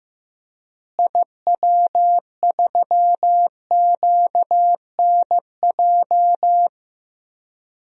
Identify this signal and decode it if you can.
Morse code, heard as a keyed tone, about 16 wpm, 700 Hz. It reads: IW3QNJ
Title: DXCC-100 Call Signs - 6 Characters - Repeated 3x - 15wpm